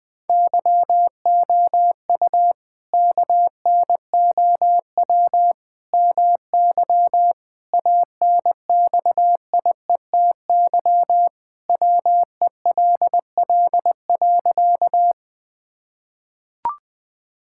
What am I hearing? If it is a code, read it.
Morse code, heard as a keyed tone, about 20 wpm, 700 Hz. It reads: YOU KNOW MY ANXIETY WELL.
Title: Sentences from Top 1600 Words 20wpm